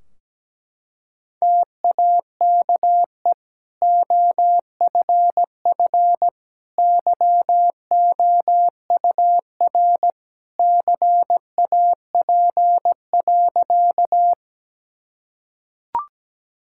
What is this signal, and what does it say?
Morse code, heard as a keyed tone, about 17 wpm, 700 Hz. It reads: TAKE OFF YOUR CAP.
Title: Sentences from Top 1600 Words 17wpm